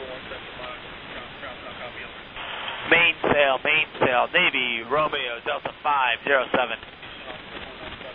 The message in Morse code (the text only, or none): none